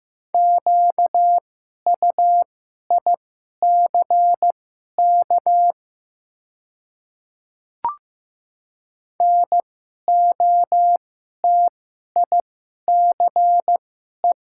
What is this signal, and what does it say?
Morse code, heard as a keyed tone, about 15 wpm, spaced as for 11 wpm, 700 Hz. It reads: QUICK NOTICE
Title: 2x Character Spacing - Top 500 Words - 15wpm